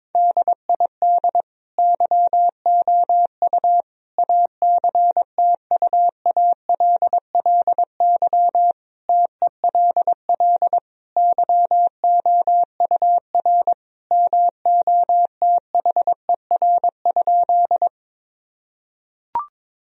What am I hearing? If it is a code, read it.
Morse code, heard as a keyed tone, about 22 wpm, 700 Hz. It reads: DID YOU ACTUALLY TELL YOUR MOTHER?
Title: Sentences from Top 1000 Words - Review Part 4 - 22wpm